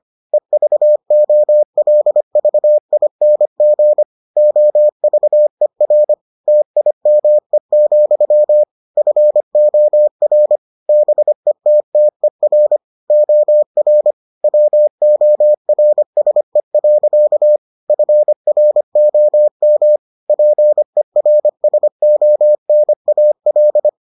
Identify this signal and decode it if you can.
Morse code, heard as a keyed tone, about 25 wpm, 600 Hz. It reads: EVOLVING OVER TIME, FOR BETTER OR WORSE. FROM PERSONAL